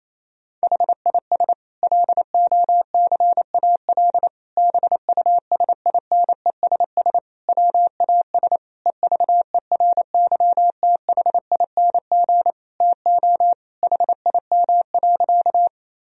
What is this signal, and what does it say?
Morse code, heard as a keyed tone, about 28 wpm, 700 Hz. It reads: HIS LOCAL BUSINESS WAS EVERYTHING TO HIM.
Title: Sentences from Top 1000 Words - Review Part 3 - 28wpm